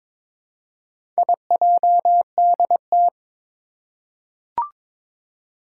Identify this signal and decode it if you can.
Morse code, heard as a keyed tone, about 22 wpm, 700 Hz. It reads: IJDT
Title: Four Challenging Letter-Number Combinations 22wpm